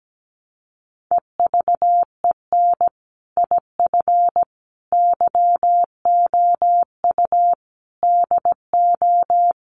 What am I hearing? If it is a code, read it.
Morse code, heard as a keyed tone, about 17 wpm, 700 Hz. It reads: EVEN IF YOU DO
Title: Sets of 5 Words Encore 17wpm